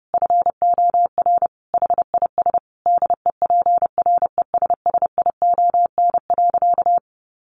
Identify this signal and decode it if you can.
Morse code, heard as a keyed tone, about 28 wpm, 700 Hz. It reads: FOR HIS DEPRESSION.
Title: Sentences from Top 1300 Words 30wpm